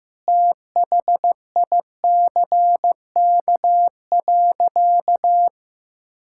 Morse code, heard as a keyed tone, about 15 wpm, 700 Hz. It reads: THICK.
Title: Sentences from Top 1600 Words 15wpm